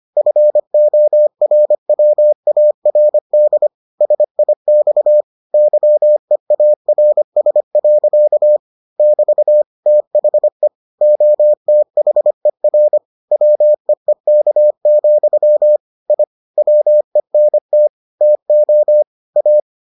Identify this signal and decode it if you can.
Morse code, heard as a keyed tone, about 25 wpm, 600 Hz. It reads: FORWARD SIX YEARS. <BT> THE OTHER WEEK, I WENT TO A